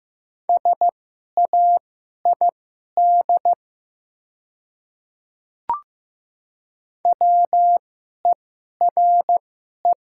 Morse code, heard as a keyed tone, about 15 wpm, spaced as for 11 wpm, 700 Hz. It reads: SAID WERE
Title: 2x Character Spacing - Top 100 Words - 15wpm